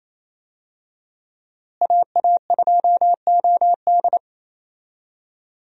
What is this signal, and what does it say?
Morse code, heard as a keyed tone, about 27 wpm, 700 Hz. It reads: AA2OD
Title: USA Call Signs - 5 Characters - Repeated 3x - 28wpm